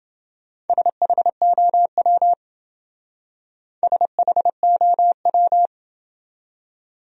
Morse code, heard as a keyed tone, about 27 wpm, 700 Hz. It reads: SHOW SHOW